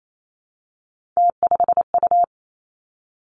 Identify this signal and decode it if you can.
Morse code, heard as a keyed tone, about 28 wpm, 700 Hz. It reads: T5U
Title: Three Challenging Letter-Number Combinations 28wpm